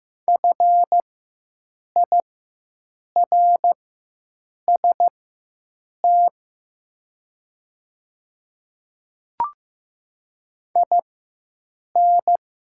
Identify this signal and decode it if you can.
Morse code, heard as a keyed tone, about 15 wpm, spaced as for 7 wpm, 700 Hz. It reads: FIRST IN
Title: 4x Character Spacing - Top 200 Words - 15wpm